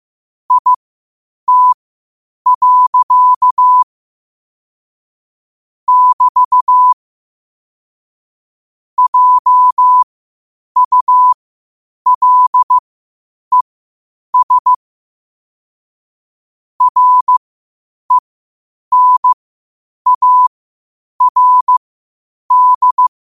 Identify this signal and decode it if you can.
Morse code, heard as a keyed tone, about 15 wpm, spaced as for 8 wpm, 1000 Hz. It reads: IT. <BT> JULES RENARD
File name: qotdfiles / 8wpm